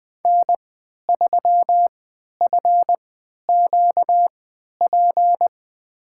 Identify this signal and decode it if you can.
Morse code, heard as a keyed tone, about 20 wpm, spaced as for 11 wpm, 700 Hz. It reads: N3FQP
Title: USA Call Signs - 5 Characters - 3x Character Spacing - 20wpm